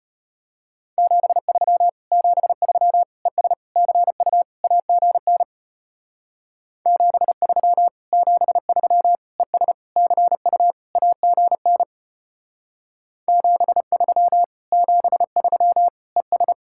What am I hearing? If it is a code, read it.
Morse code, heard as a keyed tone, about 32 wpm, 700 Hz. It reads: 73 73 ES CU AGN 73 73 ES CU AGN 73 73 ES